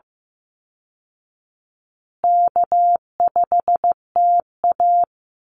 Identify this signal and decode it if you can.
Morse code, heard as a keyed tone, about 15 wpm, 700 Hz. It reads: K5TA
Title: Call Signs Repeated 2x - 15wpm